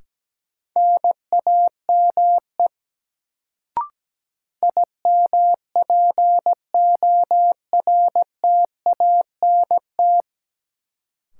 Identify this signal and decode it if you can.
Morse code, heard as a keyed tone, about 17 wpm, 700 Hz. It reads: NAME IMPORTANT
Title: Sets of 3 Words - New Word Drill - 17wpm